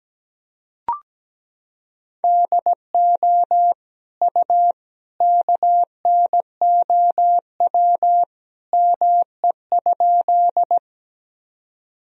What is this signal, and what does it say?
Morse code, heard as a keyed tone, about 17 wpm, 700 Hz. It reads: DO U KNOW ME?
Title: Sentences from Top 200 Words and Abbreviations 17wpm